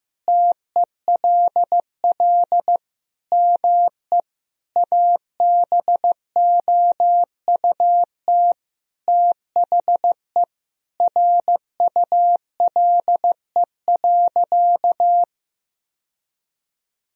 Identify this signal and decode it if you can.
Morse code, heard as a keyed tone, about 15 wpm, 700 Hz. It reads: TELL ME ABOUT THE RULE.
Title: Sentences from Top 1000 Words - Review Part 2 - 15wpm